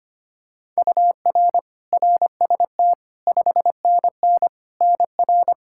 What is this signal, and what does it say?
Morse code, heard as a keyed tone, about 25 wpm, 700 Hz. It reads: UR RST 5NN NR
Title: Practice QSO - Exchange 1 - Beginning - 25wpm